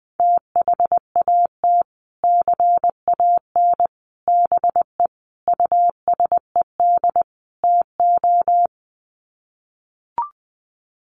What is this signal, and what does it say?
Morse code, heard as a keyed tone, about 20 wpm, 700 Hz. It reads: THAT CAN BE USED TO
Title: Sets of 5 Words Encore 20wpm